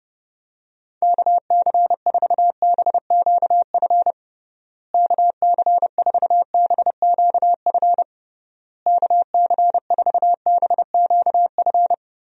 Morse code, heard as a keyed tone, about 29 wpm, 700 Hz. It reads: KC4BQF KC4BQF KC4BQF
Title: USA Call Signs - 6 Characters - Repeated 3x - 30wpm